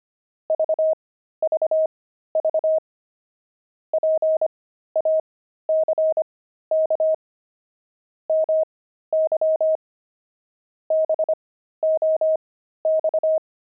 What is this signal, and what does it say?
Morse code, heard as a keyed tone, about 25 wpm, spaced as for 13 wpm, 630 Hz. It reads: VVV PACK MY BOX
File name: pangram-fs-13-cs-25-jugs